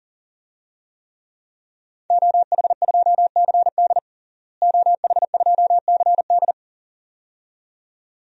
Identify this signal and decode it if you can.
Morse code, heard as a keyed tone, about 40 wpm, 700 Hz. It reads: OH2CD OH2CD
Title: DXCC-100 Call Signs - 5 Characters - Repeated 2x - 40wpm